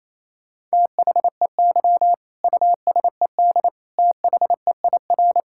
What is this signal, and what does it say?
Morse code, heard as a keyed tone, about 27 wpm, 690 Hz. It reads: THEY USED THEIR
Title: Sentences from Top 1000 Words - Review Part 6 - 28wpm